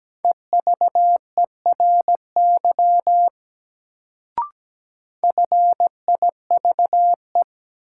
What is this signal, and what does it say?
Morse code, heard as a keyed tone, about 17 wpm, 700 Hz. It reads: EVERY FIVE
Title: Sets of 5 Words - New Word Drill - 17wpm